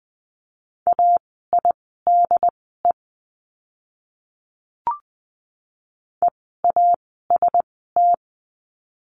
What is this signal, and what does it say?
Morse code, heard as a keyed tone, about 20 wpm, spaced as for 14 wpm, 700 Hz. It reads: AIDE EAST
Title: 068 Lesson 12f - Letter d - Words 20wpm